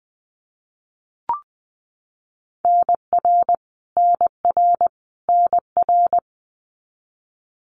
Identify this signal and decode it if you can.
Morse code, heard as a keyed tone, about 20 wpm, 700 Hz. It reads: NR NR NR
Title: Abbreviations Repeated 3x - 20wpm